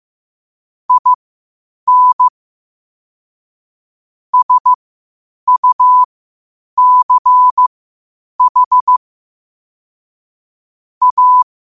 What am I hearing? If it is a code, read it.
Morse code, heard as a keyed tone, about 15 wpm, spaced as for 8 wpm, 1000 Hz. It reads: IN SUCH A
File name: qotdfiles / 8wpm